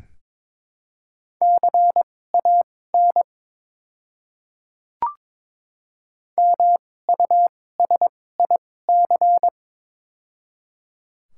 Morse code, heard as a keyed tone, about 23 wpm, spaced as for 16 wpm, 700 Hz. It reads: CAN MUSIC
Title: 2x Character Spacing - Top 300 Words - 22wpm